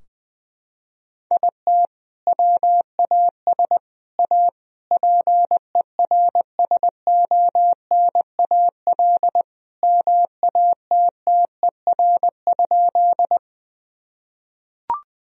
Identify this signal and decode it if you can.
Morse code, heard as a keyed tone, about 20 wpm, 700 Hz. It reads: IT WAS A PERSONAL MATTER?
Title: Sentences from Top 1000 Words - Review Part 4 - 20wpm